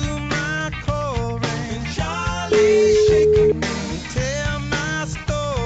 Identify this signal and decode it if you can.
Ringback tone (Europe, 425 Hz), with noise.